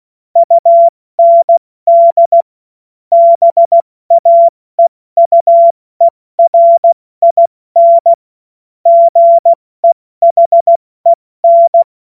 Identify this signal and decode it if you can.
Morse code, heard as a keyed tone, about 16 wpm, 680 Hz. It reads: UND BAEUERIN GEHEN
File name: Morsekurs / Witze / Witz 13 Mai 2025 70 F80